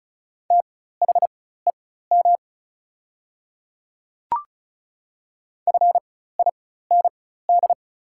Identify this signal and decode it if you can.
Morse code, heard as a keyed tone, about 36 wpm, spaced as for 16 wpm, 700 Hz. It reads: THEM FIND
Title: 4x Character Spacing - Top 200 Words - 35wpm